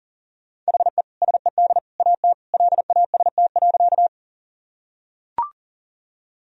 Morse code, heard as a keyed tone, about 37 wpm, 700 Hz. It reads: HE SED AT LAST.
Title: Sentences from Top 300 Words and Abbreviations 40wpm